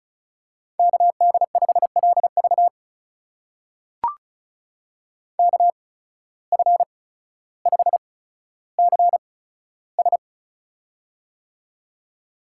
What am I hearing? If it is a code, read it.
Morse code, heard as a keyed tone, about 36 wpm, spaced as for 15 wpm, 700 Hz. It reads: KD5LV KF5CS
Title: USA Call Signs - 5 Characters - 8x Character Spacing - 35wpm